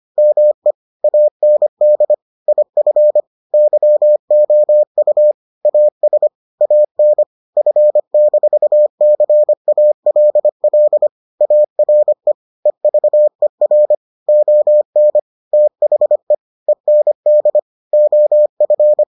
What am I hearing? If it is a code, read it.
Morse code, heard as a keyed tone, about 25 wpm, 600 Hz. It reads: ME AND IF YOU AS AN F-CALL ARE EVER ON THE END OF